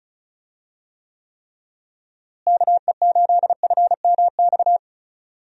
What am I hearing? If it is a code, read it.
Morse code, heard as a keyed tone, about 36 wpm, 700 Hz. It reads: KE8FMX
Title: USA Call Signs - 6 Characters - Repeated 3x - 35wpm